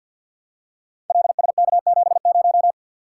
Morse code, heard as a keyed tone, about 45 wpm, 700 Hz. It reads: LSK70
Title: Code Groups - 5 Letter:Numbers - 50wpm